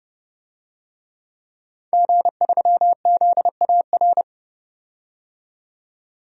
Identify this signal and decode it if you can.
Morse code, heard as a keyed tone, about 32 wpm, 700 Hz. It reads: G3ZAR
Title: DXCC-100 Call Signs - 5 Characters - 30wpm